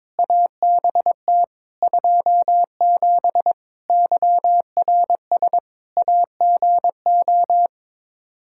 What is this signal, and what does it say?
Morse code, heard as a keyed tone, about 22 wpm, 700 Hz. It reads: ABT 27 YRS AGO
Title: Practice QSO - Exchange 3 - More Info - 22wpm